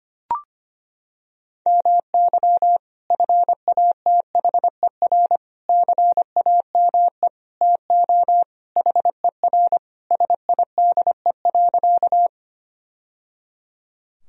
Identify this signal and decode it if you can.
Morse code, heard as a keyed tone, about 25 wpm, 700 Hz. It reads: MY FATHER CAME TO HER SIDE.
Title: Sentences from Top 1000 Words - Review Part 1 - 25wpm